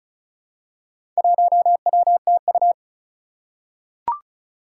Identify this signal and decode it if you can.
Morse code, heard as a keyed tone, about 36 wpm, 700 Hz. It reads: 1WTU
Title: Four Challenging Letter-Number Combinations 35wpm